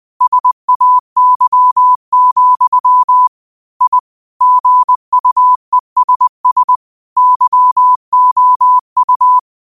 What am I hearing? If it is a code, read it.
Morse code, heard as a keyed tone, about 20 wpm, 1000 Hz. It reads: SAY, I GUESS YOU
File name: qotdfiles / 20wpm